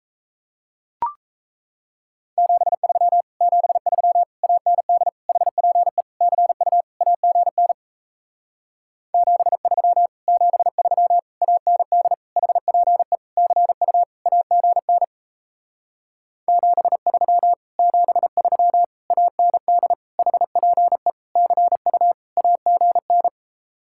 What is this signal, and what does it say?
Morse code, heard as a keyed tone, about 38 wpm, 700 Hz. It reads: 73 73 AND HPE CU AGN 73 73 AND HPE CU AGN 73 73 AND HPE CU AGN